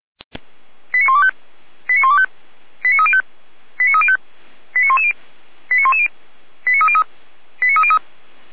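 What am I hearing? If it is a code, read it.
Five-tone selective call (ZVEI), heard as quick runs of five tones, tones of about 70 ms. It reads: 89125 89125 89305 89305 89110 89110 89303 89303
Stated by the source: ZVEI five-tone call 89276 received by SDR, fire brigade Feuerwehr Orlamünde.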